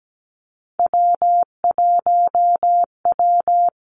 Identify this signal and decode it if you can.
Morse code, heard as a keyed tone, about 17 wpm, 700 Hz. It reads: W1W
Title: Three Challenging Letter-Number Combinations 17wpm